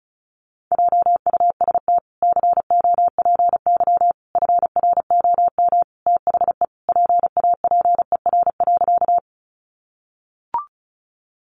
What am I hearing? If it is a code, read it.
Morse code, heard as a keyed tone, about 34 wpm, 700 Hz. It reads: JUST COPY FROM THE PAPER.
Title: Sentences from Top 1400 Words 35wpm